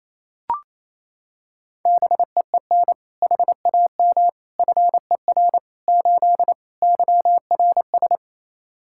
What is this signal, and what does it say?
Morse code, heard as a keyed tone, about 28 wpm, 700 Hz. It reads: BEEN HAM FER 8 YRS
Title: Practice QSO - Exchange 3 - More Info - 28wpm